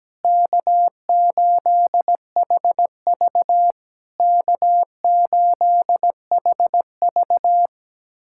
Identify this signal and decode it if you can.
Morse code, heard as a keyed tone, about 17 wpm, 700 Hz. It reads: K8HV K8HV
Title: SOTA QSO Practice 17wpm